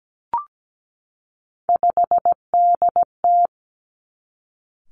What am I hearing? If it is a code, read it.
Morse code, heard as a keyed tone, about 17 wpm, 700 Hz. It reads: 5DT
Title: Three Challenging Letter-Number Combinations 17wpm